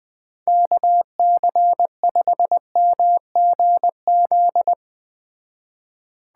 Morse code, heard as a keyed tone, about 21 wpm, 700 Hz. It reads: KC5MGZ